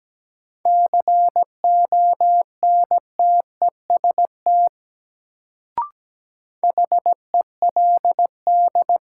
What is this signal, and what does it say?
Morse code, heard as a keyed tone, about 17 wpm, 700 Hz. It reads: CONTEST HELD
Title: Sentences from Top 1600 Words - New Word Drill Alt - 17wpm